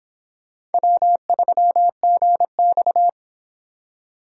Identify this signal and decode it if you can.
Morse code, heard as a keyed tone, about 28 wpm, 700 Hz. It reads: W3GX